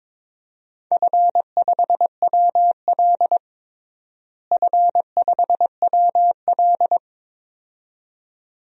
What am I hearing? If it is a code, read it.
Morse code, heard as a keyed tone, about 22 wpm, 700 Hz. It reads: F5WL F5WL
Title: DXCC-100 Call Signs - 4 Characters - Repeated 2x - 22wpm